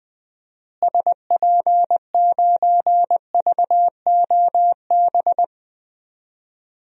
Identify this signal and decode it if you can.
Morse code, heard as a keyed tone, about 20 wpm, 700 Hz. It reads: SP9VOB
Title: DXCC-100 Call Signs - 6 Characters - Repeated 2x - 20wpm